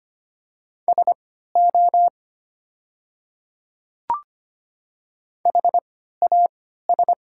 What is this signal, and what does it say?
Morse code, heard as a keyed tone, about 25 wpm, spaced as for 14 wpm, 700 Hz. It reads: SO HAS
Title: 3x Character Spacing - Top 100 Words - 25wpm